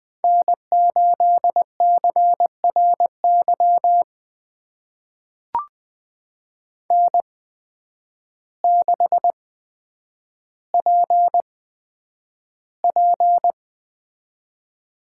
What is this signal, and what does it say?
Morse code, heard as a keyed tone, about 20 wpm, spaced as for 9 wpm, 700 Hz. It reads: N8CRY N6PP
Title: USA Call Signs - 8x Character Spacing - 20wpm